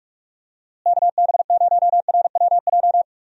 Morse code, heard as a keyed tone, about 42 wpm, 700 Hz. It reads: KB0RWJ
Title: USA Call Signs - 6 Characters - Repeated 2x - 45wpm